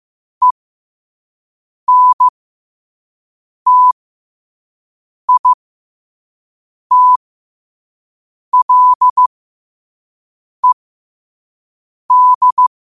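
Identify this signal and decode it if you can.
Morse code, heard as a keyed tone, about 15 wpm, spaced as for 5 wpm, 1000 Hz. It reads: ENTITLED
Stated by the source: Morse code at 5 wpm.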